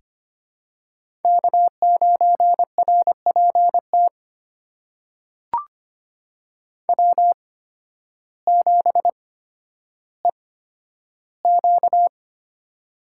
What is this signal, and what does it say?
Morse code, heard as a keyed tone, about 25 wpm, spaced as for 12 wpm, 700 Hz. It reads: K9RPT W7EQ
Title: USA Call Signs - 8x Character Spacing - 25wpm